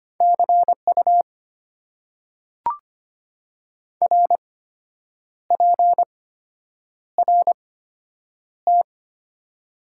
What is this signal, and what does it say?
Morse code, heard as a keyed tone, about 25 wpm, spaced as for 8 wpm, 690 Hz. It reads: CU RPRT
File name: Top 90 QSO Elements - 8x Character Spacing - 25wpm